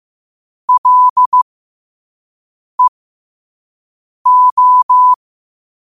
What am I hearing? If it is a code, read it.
Morse code, heard as a keyed tone, about 15 wpm, spaced as for 5 wpm, 1000 Hz. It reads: LEO
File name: qotdfiles / 5wpm